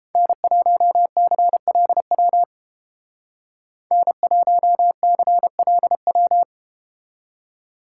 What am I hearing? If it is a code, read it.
Morse code, heard as a keyed tone, about 31 wpm, 700 Hz. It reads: N1CLW N1CLW